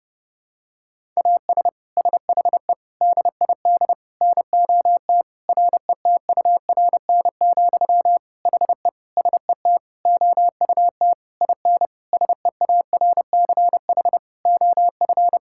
Morse code, heard as a keyed tone, about 30 wpm, 700 Hz. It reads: AS SHE DID NOT RETURN, HE SET OUT IN SEARCH OF
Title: Sentences from Top 1000 Words - Review Part 5 - 30wpm